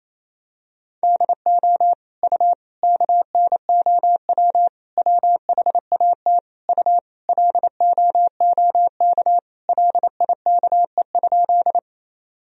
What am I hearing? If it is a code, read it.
Morse code, heard as a keyed tone, about 28 wpm, 700 Hz. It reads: DO U KNOW WHAT U LOOK LIKE?
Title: Sentences from Top 100 Words and Abbreviations 28wpm